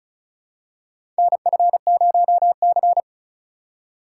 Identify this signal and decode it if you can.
Morse code, heard as a keyed tone, about 36 wpm, 700 Hz. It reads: NF0C
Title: USA Call Signs - 4 Characters - 3x Character Spacing - 35wpm